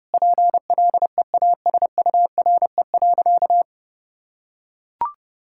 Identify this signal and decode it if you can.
Morse code, heard as a keyed tone, about 32 wpm, 700 Hz. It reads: PLEASURE.
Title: Sentences from Top 1600 Words 30wpm